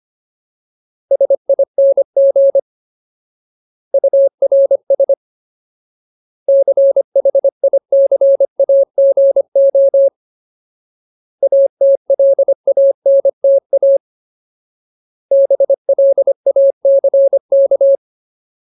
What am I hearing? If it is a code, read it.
Morse code, heard as a keyed tone, about 25 wpm, 550 Hz. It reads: SING URS CHICAGO ATLANTA BLACK
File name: Ultimate Word List / 25WPM Ultimate